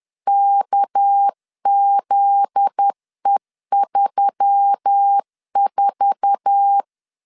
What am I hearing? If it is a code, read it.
Morse code, heard as a keyed tone, about 10 wpm, 800 Hz. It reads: KZE34
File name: LEZIONI / 8 YJ45 / 8 YJ4 10wpm 2633